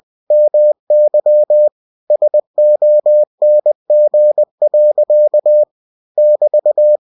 Morse code, heard as a keyed tone, about 20 wpm, 600 Hz. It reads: MY SONG. <BT>
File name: winnie-20wpm